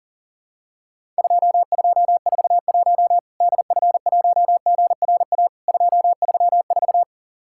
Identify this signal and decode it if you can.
Morse code, heard as a keyed tone, about 39 wpm, 700 Hz. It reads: 2241 DF1GRA 234
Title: Worked All Europe DX Contest - QTC Exchange - 40wpm